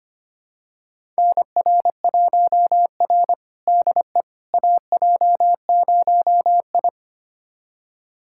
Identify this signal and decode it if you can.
Morse code, heard as a keyed tone, about 25 wpm, 700 Hz. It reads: NR1R DE AJ0I
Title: USA Callsign DE Callsign - 4 Characters - 25wpm